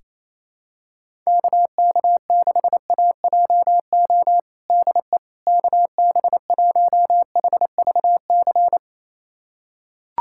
Morse code, heard as a keyed tone, about 28 wpm, 700 Hz. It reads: KK6AJO DE KB1HVC
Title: USA Callsign DE Callsign 28wpm